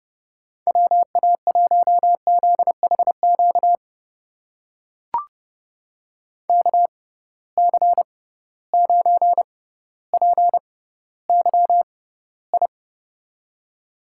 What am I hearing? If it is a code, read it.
Morse code, heard as a keyed tone, about 30 wpm, spaced as for 15 wpm, 700 Hz. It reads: WA1ZHQ KC9PYI
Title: USA Call Signs - 6 Characters - 6x Character Spacing - 30wpm